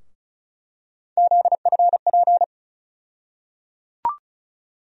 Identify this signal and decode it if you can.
Morse code, heard as a keyed tone, about 36 wpm, 700 Hz. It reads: ZFP
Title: USA Call Sign Postfixes - 3 Letters 35wpm